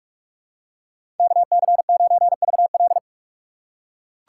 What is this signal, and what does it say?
Morse code, heard as a keyed tone, about 41 wpm, 700 Hz. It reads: KC9VL
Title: USA Call Signs - 5 Characters - 3x Character Spacing - 45wpm